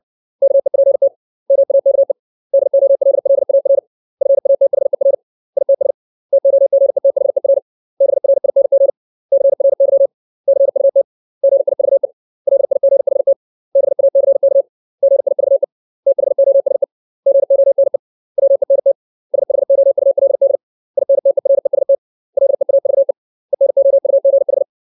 Morse code, heard as a keyed tone, about 60 wpm, 550 Hz. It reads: KEPT MAKE DOLLAR FATHER ITS TOGETHER BRING CRY CUT GIVE LIGHT BACK GIVE THOSE GONE WENT SHOULD INTEREST LEAVE ENOUGH
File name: 632 words / 632 Words 60WPM